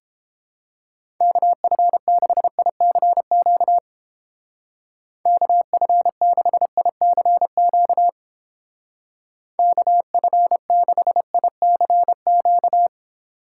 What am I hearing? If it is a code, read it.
Morse code, heard as a keyed tone, about 31 wpm, 700 Hz. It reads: KF6ICQ KF6ICQ KF6ICQ